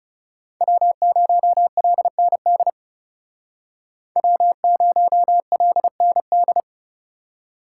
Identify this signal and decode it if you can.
Morse code, heard as a keyed tone, about 35 wpm, 700 Hz. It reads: W0LND W0LND
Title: Speed Racing - USA 1x3 Call Signs 30wpm